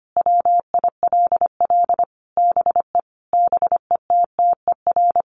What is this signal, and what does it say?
Morse code, heard as a keyed tone, about 25 wpm, 700 Hz. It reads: WILL BE BETTER
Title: Sentences from Top 1500 Words 25wpm